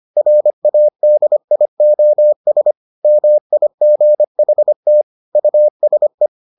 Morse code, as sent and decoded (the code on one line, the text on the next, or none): .-. .- -.. .. --- ... / -- .. --. .... - / ..- ... .
RADIOS MIGHT USE